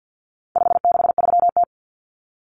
Morse code, heard as a keyed tone, about 50 wpm, 700 Hz. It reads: 563T
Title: Common Cut Numbers - 4-Digit 50wpm